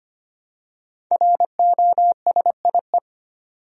Morse code, heard as a keyed tone, about 25 wpm, 700 Hz. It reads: ROSIE
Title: USA Common Names - 25wpm